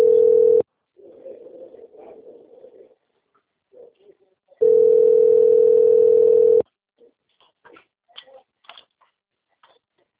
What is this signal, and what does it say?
Ringback tone (North America), 440 + 480 Hz.